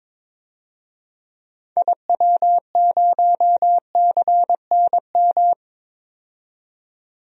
Morse code, heard as a keyed tone, about 22 wpm, 700 Hz. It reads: IW0CNM
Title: DXCC-100 Call Signs - 6 Characters - 22wpm